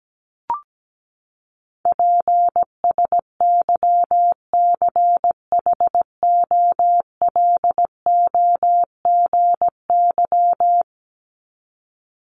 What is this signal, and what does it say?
Morse code, heard as a keyed tone, about 17 wpm, 700 Hz. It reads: PSYCHOLOGY
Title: Top 1251-1500 Words 17wpm